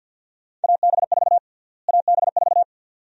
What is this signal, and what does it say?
Morse code, heard as a keyed tone, about 46 wpm, 700 Hz. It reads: AB4 AB4
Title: USA Call Sign Prefixes - Repeated 2x - 50wpm